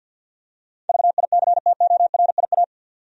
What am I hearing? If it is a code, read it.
Morse code, heard as a keyed tone, about 52 wpm, 700 Hz. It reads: VICTORIA
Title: USA Common Names - 50wpm